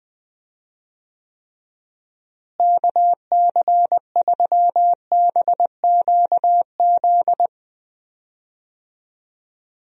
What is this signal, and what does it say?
Morse code, heard as a keyed tone, about 20 wpm, 700 Hz. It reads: KC3BQZ
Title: USA Call Signs - 6 Characters - Repeated 2x - 20wpm